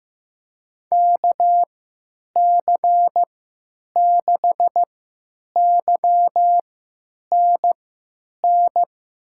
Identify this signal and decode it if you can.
Morse code, heard as a keyed tone, about 15 wpm, spaced as for 9 wpm, 700 Hz. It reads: KC6YNN
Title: USA Call Signs - 6 Characters - 3x Character Spacing - 15wpm